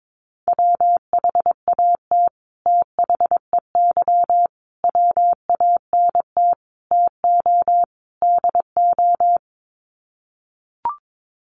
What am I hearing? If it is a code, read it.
Morse code, heard as a keyed tone, about 22 wpm, 700 Hz. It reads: WHAT THEY WANT TO DO
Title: Sets of 5 Words Encore2 22wpm